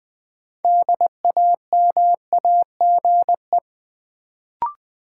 Morse code, heard as a keyed tone, about 20 wpm, 700 Hz. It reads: DAMAGE
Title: Sentences from Top 1100 Words - New Word Drill - 20wpm